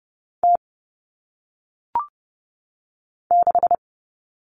Morse code, heard as a keyed tone, about 30 wpm, 700 Hz. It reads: T 6
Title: Common Cut Numbers - 1-Digit 30wpm